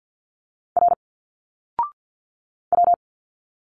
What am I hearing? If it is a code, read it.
Morse code, heard as a keyed tone, about 52 wpm, 700 Hz. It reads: R W
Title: Single Letters 50wpm